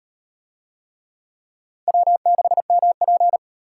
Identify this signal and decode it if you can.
Morse code, heard as a keyed tone, about 37 wpm, 700 Hz. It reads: W6MP